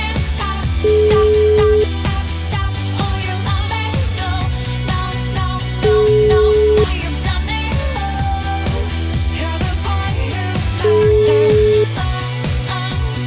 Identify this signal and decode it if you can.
Ringback tone (Europe, 425 Hz), with noise.